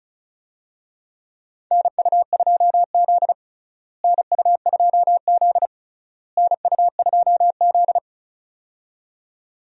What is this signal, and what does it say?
Morse code, heard as a keyed tone, about 36 wpm, 700 Hz. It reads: NU2Z NU2Z NU2Z
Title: USA Call Signs - 4 Characters - Repeated 3x - 35wpm